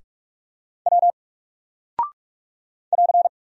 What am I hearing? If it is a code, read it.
Morse code, heard as a keyed tone, about 44 wpm, 700 Hz. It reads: W <AR>
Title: Single Character (Letter:Number:Punctuation:Prosign) 45wpm